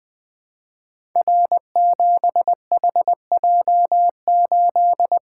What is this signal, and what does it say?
Morse code, heard as a keyed tone, about 20 wpm, 700 Hz. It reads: R7HJ8
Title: Code Groups - 5 Letter:Numbers - 20wpm